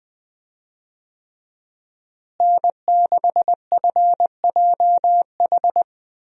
Morse code, heard as a keyed tone, about 20 wpm, 700 Hz. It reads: N6FJH
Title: USA Call Signs - 5 Characters - Repeated 3x - 20wpm